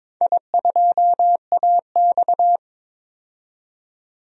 Morse code, heard as a keyed tone, about 22 wpm, 700 Hz. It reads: I2AX
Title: DXCC-100 Call Signs - 4 Characters - 22wpm